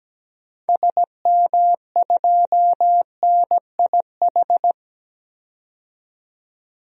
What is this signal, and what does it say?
Morse code, heard as a keyed tone, about 17 wpm, 700 Hz. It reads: SM2NIH
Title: DXCC-100 Call Signs - 6 Characters - 17wpm